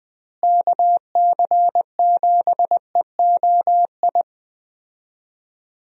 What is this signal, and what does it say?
Morse code, heard as a keyed tone, about 20 wpm, 700 Hz. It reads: KC7EOI
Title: USA Call Signs - 6 Characters - Repeated 2x - 20wpm